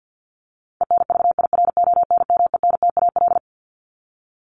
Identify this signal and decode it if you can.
Morse code, heard as a keyed tone, about 50 wpm, 700 Hz. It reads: ENVIRONMENTAL
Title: Top 1001-1250 Words 50wpm